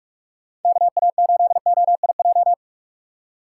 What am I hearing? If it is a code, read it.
Morse code, heard as a keyed tone, about 41 wpm, 700 Hz. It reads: KA8OIJ